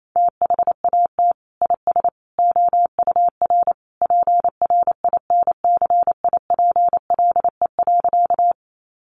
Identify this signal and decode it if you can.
Morse code, heard as a keyed tone, about 28 wpm, 700 Hz. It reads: THAT IS OUR PRINCIPLE.
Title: Sentences from Top 1400 Words 28wpm